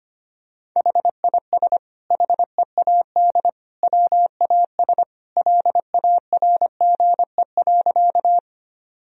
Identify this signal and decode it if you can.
Morse code, heard as a keyed tone, about 25 wpm, 700 Hz. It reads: HIS HEAD WAS LARGE.
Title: Sentences from Top 1000 Words - Review Part 1 - 25wpm